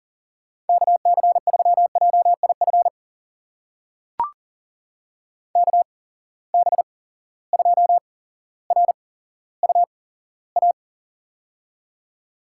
Morse code, heard as a keyed tone, about 40 wpm, spaced as for 17 wpm, 700 Hz. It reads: KC3JIF KB2RUA
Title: USA Call Signs - 6 Characters - 8x Character Spacing - 40wpm